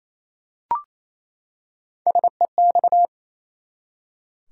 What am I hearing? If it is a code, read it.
Morse code, heard as a keyed tone, about 28 wpm, 700 Hz. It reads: SEX
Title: Three Letter Words - 28wpm